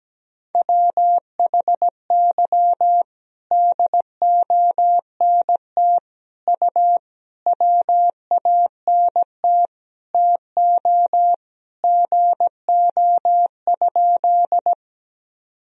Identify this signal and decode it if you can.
Morse code, heard as a keyed tone, about 17 wpm, 700 Hz. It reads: WHY DONT U WANT TO GO?
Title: Sentences from Top 300 Words and Abbreviations 17wpm